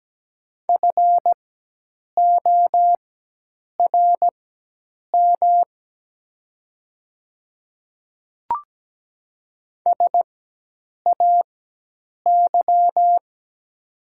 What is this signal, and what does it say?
Morse code, heard as a keyed tone, about 17 wpm, spaced as for 8 wpm, 700 Hz. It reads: FORM SAY
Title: 4x Character Spacing - Top 200 Words - 17wpm